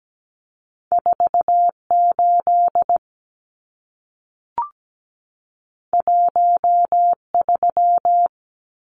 Morse code, heard as a keyed tone, about 17 wpm, 700 Hz. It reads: 48 13
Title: Numbers - 2-Digit 17wpm